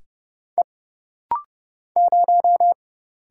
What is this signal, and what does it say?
Morse code, heard as a keyed tone, about 30 wpm, 700 Hz. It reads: E 0
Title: Single Letter-Number Alt - Rapid-Fire 30wpm